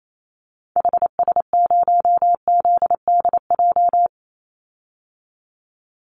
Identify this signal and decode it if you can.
Morse code, heard as a keyed tone, about 28 wpm, 700 Hz. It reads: HS0ZDJ
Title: Call Signs Repeated 3x - 28wpm